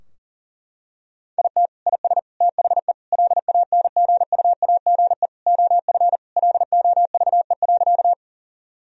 Morse code, heard as a keyed tone, about 40 wpm, 700 Hz. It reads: IT IS THE LANGUAGE OF LOVE.
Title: Sentences from Top 1000 Words - Review Part 3 - 40wpm